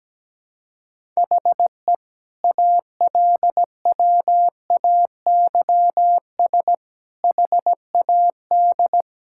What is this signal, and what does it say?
Morse code, heard as a keyed tone, about 17 wpm, 700 Hz. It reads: HE ALWAYS HAD
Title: Sentences from Top 1600 Words 17wpm